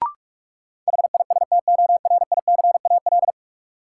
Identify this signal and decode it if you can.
Morse code, heard as a keyed tone, about 46 wpm, 700 Hz. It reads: HISTORICAL
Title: Sentences from Top 600 Words - New Word Drill - 45wpm